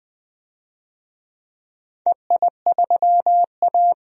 Morse code, heard as a keyed tone, about 20 wpm, 700 Hz. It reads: EI3A
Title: DXCC-100 Call Signs - 4 Characters - 20wpm